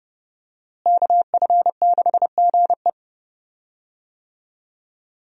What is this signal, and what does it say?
Morse code, heard as a keyed tone, about 29 wpm, 700 Hz. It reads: KF6GE
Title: USA Call Signs - 5 Characters - Repeated 2x - 30wpm